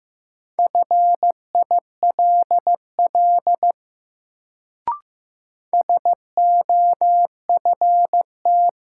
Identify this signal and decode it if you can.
Morse code, heard as a keyed tone, about 15 wpm, 700 Hz. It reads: FILL SOFT
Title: Sentences from Top 1500 Words - New Word Drill Alt - 15wpm